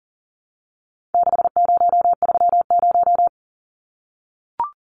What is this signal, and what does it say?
Morse code, heard as a keyed tone, about 40 wpm, 700 Hz. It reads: 6030
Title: Numbers - 4-Digit 40wpm